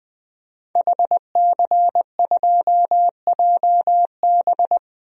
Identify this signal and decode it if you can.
Morse code, heard as a keyed tone, about 20 wpm, 700 Hz. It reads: HC2JB
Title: DXCC-100 Call Signs - 5 Characters - Repeated 2x - 20wpm